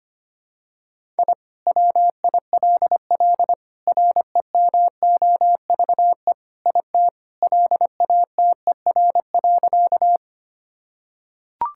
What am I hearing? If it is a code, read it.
Morse code, heard as a keyed tone, about 25 wpm, 700 Hz. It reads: I WILL REMOVE IT LATER.
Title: Sentences from Top 1000 Words - Review Part 5 - 25wpm